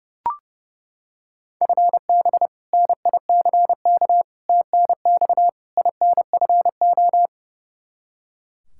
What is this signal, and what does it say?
Morse code, heard as a keyed tone, about 32 wpm, 700 Hz. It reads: FB NICK TNX INFO
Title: Practice QSO - Exchange 3 - More Info - 30wpm